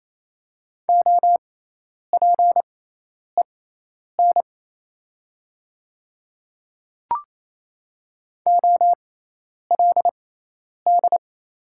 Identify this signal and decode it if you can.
Morse code, heard as a keyed tone, about 28 wpm, spaced as for 10 wpm, 690 Hz. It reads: OPEN OLD
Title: 6x Character Spacing - Top 500 Words - 28wpm